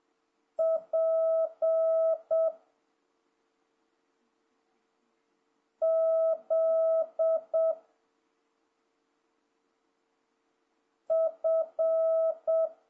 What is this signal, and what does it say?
Morse code, heard as a keyed tone, about 7 wpm, 650 Hz. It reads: P Z F
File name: jh8cbh / 2026cw